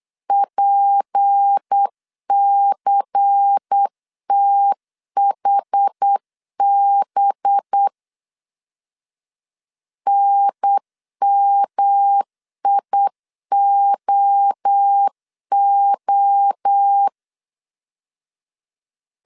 Morse code, heard as a keyed tone, about 8 wpm, 800 Hz. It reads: PCTHB NMIOO
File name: CW / DET / 1 6 1500C S B 8wpm 5142